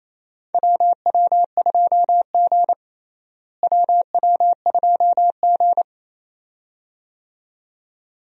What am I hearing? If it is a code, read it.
Morse code, heard as a keyed tone, about 28 wpm, 700 Hz. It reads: WW2G WW2G
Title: USA Call Signs - 4 Characters - Repeated 2x - 28wpm